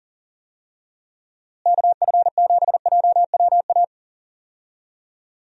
Morse code, heard as a keyed tone, about 40 wpm, 700 Hz. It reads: KF7JWA
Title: USA Call Signs - 6 Characters - Repeated 2x - 40wpm